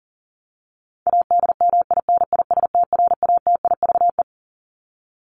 Morse code, heard as a keyed tone, about 40 wpm, 700 Hz. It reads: ADMINISTRATIVE
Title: Top 1501-1750 Words 40wpm